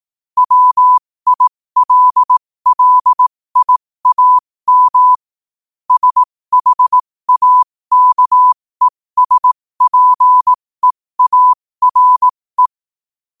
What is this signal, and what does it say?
Morse code, heard as a keyed tone, about 18 wpm, spaced as for 15 wpm, 1000 Hz. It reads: WILLIAM SHAKESPEARE
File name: qotdfiles / 15wpm